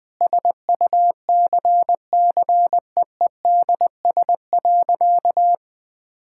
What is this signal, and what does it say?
Morse code, heard as a keyed tone, about 20 wpm, 700 Hz. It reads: SUCCEEDS.
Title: Sentences with 8-Word Phrases - 20wpm